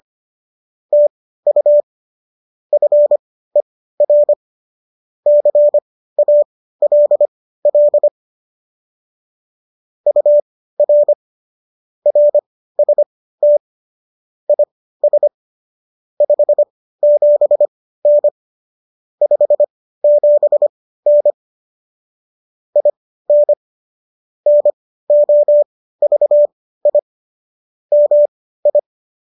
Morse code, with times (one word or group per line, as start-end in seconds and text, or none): 0.9-1.8 TU
2.7-4.3 FER
5.3-8.1 CALL
10.1-11.1 UR
12.1-13.6 RST
14.5-15.3 IS
16.2-18.3 57N
19.2-21.3 57N
22.8-23.5 IN
24.5-27.0 NOVI
27.9-28.8 MI